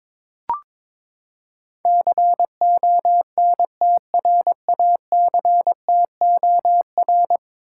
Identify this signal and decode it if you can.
Morse code, heard as a keyed tone, about 22 wpm, 700 Hz. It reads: CONTRACTOR
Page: morse-code-ninja-practice-22wpm